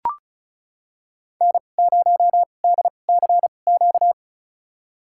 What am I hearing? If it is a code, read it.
Morse code, heard as a keyed tone, about 36 wpm, spaced as for 25 wpm, 700 Hz. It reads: N0DCQ
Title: USA Call Signs - 5 Characters - 2x Character Spacing - 35wpm